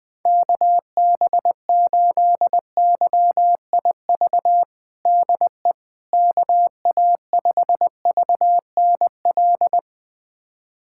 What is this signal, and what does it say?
Morse code, heard as a keyed tone, about 20 wpm, 700 Hz. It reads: KB8YIV DE KA5VNL
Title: USA Callsign DE Callsign - 6 Characters - 20wpm